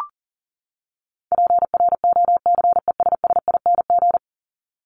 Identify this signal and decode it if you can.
Morse code, heard as a keyed tone, about 40 wpm, 700 Hz. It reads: PROCESSING
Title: ING Suffix - Top 100 Words with Base Words 40wpm